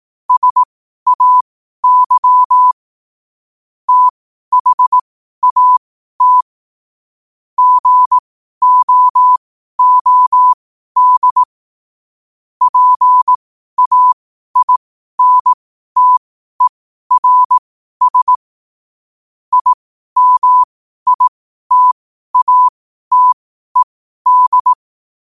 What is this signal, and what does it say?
Morse code, heard as a keyed tone, about 18 wpm, spaced as for 12 wpm, 1000 Hz. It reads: SAY THAT GOOD PAINTERS IMITATED
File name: qotdfiles / 12wpm